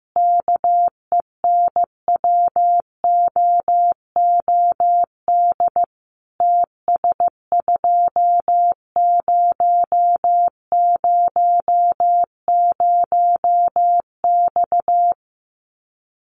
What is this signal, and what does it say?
Morse code, heard as a keyed tone, about 15 wpm, 700 Hz. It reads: KENWOOD TS2000X
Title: Common QSO Phrases 15wpm